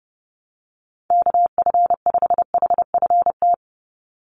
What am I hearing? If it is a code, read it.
Morse code, heard as a keyed tone, about 30 wpm, 700 Hz. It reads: KF5HFT
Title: USA Call Signs 30wpm